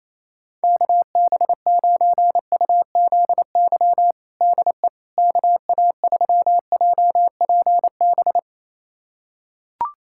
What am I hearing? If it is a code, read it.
Morse code, heard as a keyed tone, about 28 wpm, 700 Hz. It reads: KB9UZY DE KA3JPB
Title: USA Callsign DE Callsign - 6 Characters - 28wpm